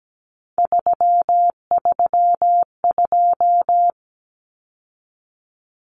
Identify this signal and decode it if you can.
Morse code, heard as a keyed tone, about 17 wpm, 700 Hz. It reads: 332
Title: Numbers - 3-Digit 17wpm